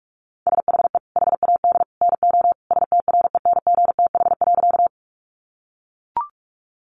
Morse code, heard as a keyed tone, about 44 wpm, 700 Hz. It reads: SHE HAD NO STRENGTH.
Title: Sentences from Top 1300 Words 45wpm